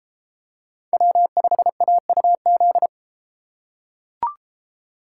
Morse code, heard as a keyed tone, about 32 wpm, 700 Hz. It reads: W5AUZ